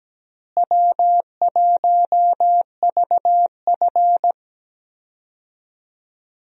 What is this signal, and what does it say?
Morse code, heard as a keyed tone, about 17 wpm, 700 Hz. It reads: W1VF
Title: USA Call Signs - 4 Characters - Repeated 3x - 17wpm